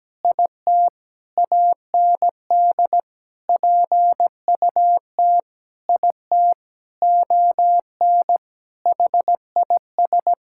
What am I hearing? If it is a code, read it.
Morse code, heard as a keyed tone, about 17 wpm, 700 Hz. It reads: IT AND PUT IT ON HIS
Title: Sentences from Top 1600 Words 17wpm